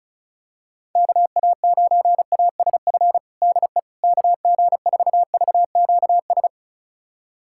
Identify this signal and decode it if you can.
Morse code, heard as a keyed tone, about 36 wpm, 700 Hz. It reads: KA9ASF DE KG4VQS
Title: USA Callsign DE Callsign 35wpm